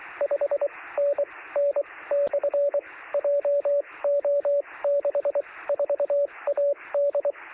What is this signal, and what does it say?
Morse code, heard as a keyed tone, about 24 wpm, spaced as for 18 wpm, 570 Hz, with radio noise. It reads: 5NN/JO64AD
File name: sm7lcb / 1856z dl0vv 13cm